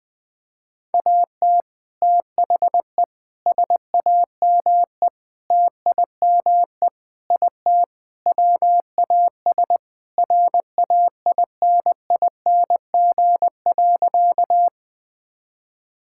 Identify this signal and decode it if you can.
Morse code, heard as a keyed tone, about 21 wpm, 700 Hz. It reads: AT THE SAME TIME IT WAS RAINING.
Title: Sentences with 3-Word Phrases - Spoken Beginning - 20wpm